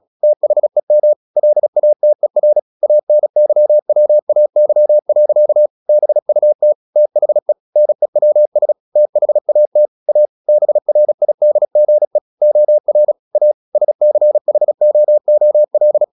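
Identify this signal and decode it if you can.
Morse code, heard as a keyed tone, about 36 wpm, 600 Hz. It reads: THEM LATER ANYWAY. BUT THE NEWS THAT A BRIDGE OR A SCHOOL
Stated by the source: Morse practice at 36 wpm, text of Ken Follett's novel Never, from CW med Gnister.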